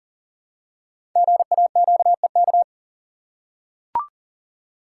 Morse code, heard as a keyed tone, about 39 wpm, 700 Hz. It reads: GAQEK
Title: Code Groups - 5 Letters - 40wpm